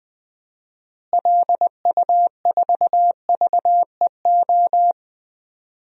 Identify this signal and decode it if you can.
Morse code, heard as a keyed tone, about 21 wpm, 700 Hz. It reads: LU4VEO
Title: DXCC-100 Call Signs - 6 Characters - Repeated 2x - 20wpm